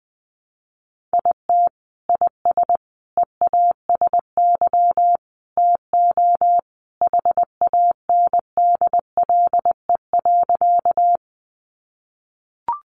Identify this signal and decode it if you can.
Morse code, heard as a keyed tone, about 20 wpm, 700 Hz. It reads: IT IS EASY TO HANDLE.
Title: Sentences from Top 1100 Words 20wpm